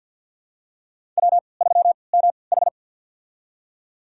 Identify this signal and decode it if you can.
Morse code, heard as a keyed tone, about 46 wpm, spaced as for 27 wpm, 700 Hz. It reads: W3MH
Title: USA Call Signs - 4 Characters - 3x Character Spacing - 50wpm